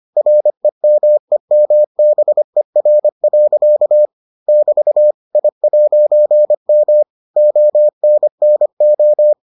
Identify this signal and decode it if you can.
Morse code, heard as a keyed tone, about 25 wpm, 600 Hz. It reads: REMEMBER. <BT> I'M ONNO